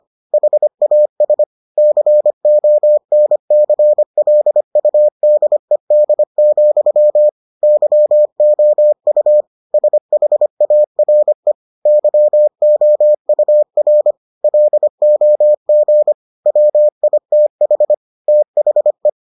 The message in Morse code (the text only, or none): HAS CONCLUDED, YOU SHARE YOUR LOG WITH THE